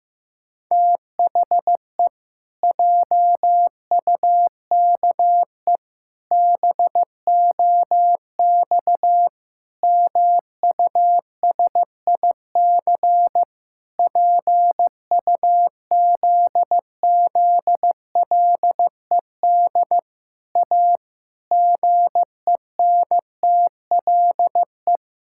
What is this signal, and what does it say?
Morse code, heard as a keyed tone, about 15 wpm, 700 Hz. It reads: THE JUKE BOX MUSIC PUZZLED A GENTLE
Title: Pangram Challenge v2 - 15wpm